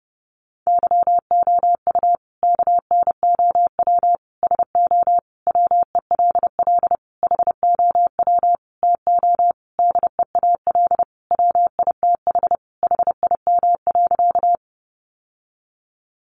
Morse code, heard as a keyed tone, about 30 wpm, 700 Hz. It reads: YOU KNOW SO WELL HOW TO DEAL WITH HIM.
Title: Sentences from Top 800 Words 30wpm